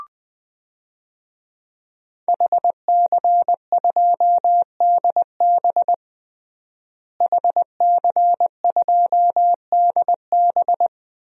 Morse code, heard as a keyed tone, about 20 wpm, 700 Hz. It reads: HC2DB HC2DB
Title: DXCC-100 Call Signs - 5 Characters - Repeated 2x - 20wpm